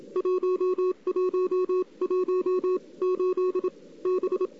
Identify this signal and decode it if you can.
Morse code, heard as a keyed tone, about 27 wpm, spaced as for 23 wpm, 370 Hz. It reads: 1118 B
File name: CW / 2008-ss-cw-k5tr-op-n5rz -02:00